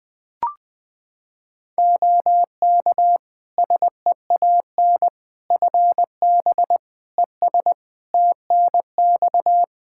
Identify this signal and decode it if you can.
Morse code, heard as a keyed tone, about 20 wpm, 700 Hz. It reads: OK SEAN FB ES TNX
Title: Practice QSO - Exchange 2 - Info - 20wpm